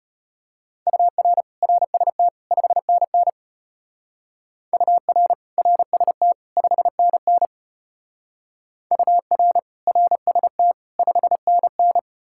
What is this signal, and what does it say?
Morse code, heard as a keyed tone, about 30 wpm, 700 Hz. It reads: UR RST 5NN UR RST 5NN UR RST 5NN
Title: Speed-Racing Common QSO Phrases Encore - 25wpm